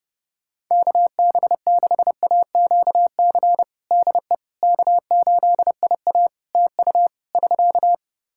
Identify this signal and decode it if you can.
Morse code, heard as a keyed tone, about 30 wpm, 700 Hz. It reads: KB6AQC DE K8IA TU <SK>
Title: K1USN SST QSO Practice 30wpm